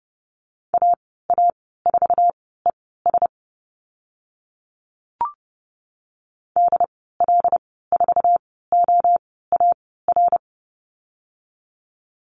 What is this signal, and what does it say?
Morse code, heard as a keyed tone, about 30 wpm, spaced as for 17 wpm, 700 Hz. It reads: AA4ES DL4OAR
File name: Lesson 14g - Number 2 - Call Signs 30wpm